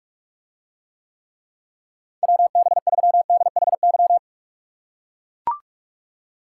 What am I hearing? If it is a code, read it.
Morse code, heard as a keyed tone, about 44 wpm, 700 Hz. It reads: WB3DHY